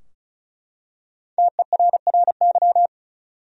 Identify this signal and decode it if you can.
Morse code, heard as a keyed tone, about 36 wpm, 700 Hz. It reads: TERRY
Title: USA Common Names - Max 5 Letters - 35wpm